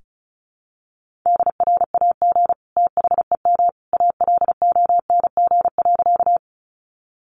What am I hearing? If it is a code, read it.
Morse code, heard as a keyed tone, about 34 wpm, 700 Hz. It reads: DRAG THEM ALONG.
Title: Sentences from Top 1500 Words 35wpm